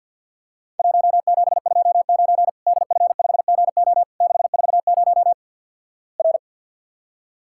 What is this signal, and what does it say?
Morse code, heard as a keyed tone, about 50 wpm, 700 Hz. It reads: 1729 DF5GO 640 R
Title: Worked All Europe DX Contest - QTC Exchange with Repeats - 50wpm